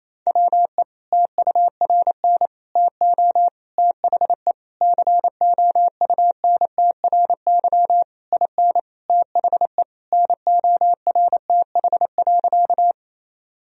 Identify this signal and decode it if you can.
Morse code, heard as a keyed tone, about 28 wpm, 700 Hz. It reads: WE TURN TO THE COUNTRY IN THE NORTH.
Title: Sentences from Top 1000 Words - Review Part 1 - 28wpm